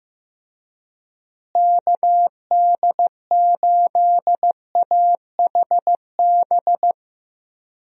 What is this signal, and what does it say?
Morse code, heard as a keyed tone, about 16 wpm, 700 Hz. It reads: KD8AHB
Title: USA Call Signs - 6 Characters - Repeated 2x - 15wpm